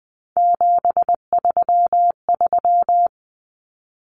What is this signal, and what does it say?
Morse code, heard as a keyed tone, about 20 wpm, 700 Hz. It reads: 733
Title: Common Cut Numbers - 3-Digit 20wpm